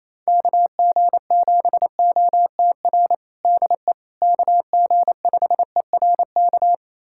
Morse code, heard as a keyed tone, about 27 wpm, 700 Hz. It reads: KG7OTR DE KG5ERK
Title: USA Callsign DE Callsign - 6 Characters - 28wpm